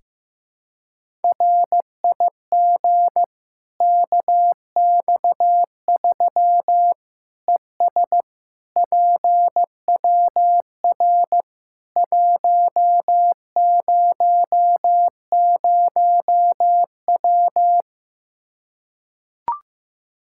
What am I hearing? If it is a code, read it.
Morse code, heard as a keyed tone, about 15 wpm, 700 Hz. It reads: RIG KX3 ES PWR 100W
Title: Practice QSO - Exchange 2 - Info - 15wpm